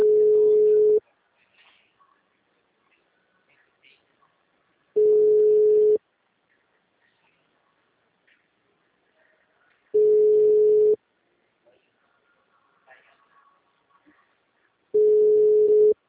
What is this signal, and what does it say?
Ringback tone (Europe, 425 Hz), with speech.